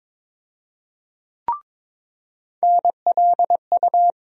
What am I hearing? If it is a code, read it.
Morse code, heard as a keyed tone, about 23 wpm, 700 Hz. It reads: NLU
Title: USA Call Sign Postfixes - 3 Letters 22wpm